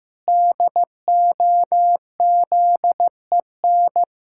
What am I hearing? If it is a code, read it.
Morse code, heard as a keyed tone, about 15 wpm, 700 Hz. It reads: DOZEN
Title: Pangram Challenge v2 - 15wpm